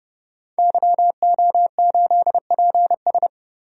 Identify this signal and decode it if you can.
Morse code, heard as a keyed tone, about 31 wpm, 700 Hz. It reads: YO8PS
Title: DXCC-100 Call Signs - 5 Characters - 30wpm